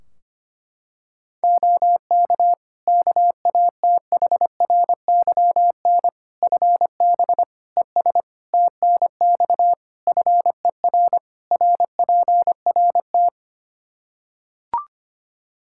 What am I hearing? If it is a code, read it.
Morse code, heard as a keyed tone, about 25 wpm, 700 Hz. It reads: OK KATHRYN FB ES TNX FER RPRT
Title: K1USN SST QSO Practice 25wpm